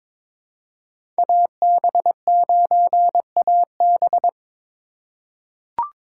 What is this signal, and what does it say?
Morse code, heard as a keyed tone, about 22 wpm, 700 Hz. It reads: AB9AB
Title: USA Call Signs - 5 Characters - 2x Character Spacing - 22wpm